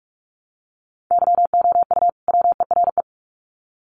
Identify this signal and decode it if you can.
Morse code, heard as a keyed tone, about 44 wpm, 700 Hz. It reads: YOU WERE
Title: Sets of 2 Words 45wpm